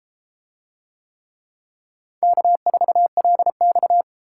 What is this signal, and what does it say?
Morse code, heard as a keyed tone, about 29 wpm, 700 Hz. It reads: K4LX